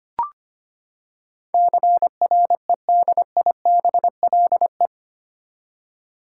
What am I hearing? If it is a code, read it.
Morse code, heard as a keyed tone, about 25 wpm, 700 Hz. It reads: CREDIBLE
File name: Eight Letter Words - 25wpm